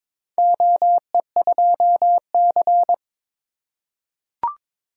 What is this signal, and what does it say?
Morse code, heard as a keyed tone, about 22 wpm, 700 Hz. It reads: OE2C
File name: practice / 22wpm / Four Letter-Number Combinations - 22wpm